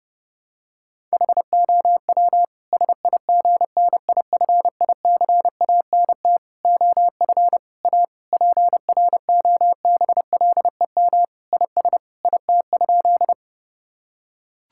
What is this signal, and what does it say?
Morse code, heard as a keyed tone, about 30 wpm, 700 Hz. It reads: HOW SIGNIFICANT OF A PROBLEM IS IT?
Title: Sentences from Top 1000 Words - Review Part 6 - 30wpm